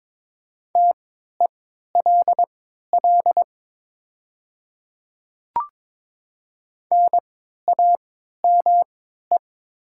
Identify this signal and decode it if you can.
Morse code, heard as a keyed tone, about 22 wpm, spaced as for 13 wpm, 700 Hz. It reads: TELL NAME
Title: 3x Character Spacing - Top 200 Words - 22wpm